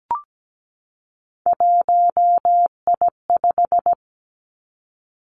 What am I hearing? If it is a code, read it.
Morse code, heard as a keyed tone, about 17 wpm, 700 Hz. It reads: 1I5
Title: Three Challenging Letter-Number Combinations 17wpm